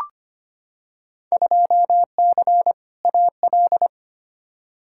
Morse code, heard as a keyed tone, about 25 wpm, 700 Hz. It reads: 2C AL
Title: Field Day Exchanges Encore - 25wpm